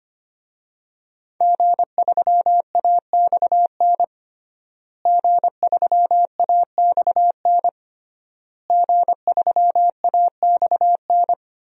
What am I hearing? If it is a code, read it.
Morse code, heard as a keyed tone, about 25 wpm, 700 Hz. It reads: G3AXN G3AXN G3AXN
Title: DXCC-100 Call Signs - 5 Characters - Repeated 3x - 25wpm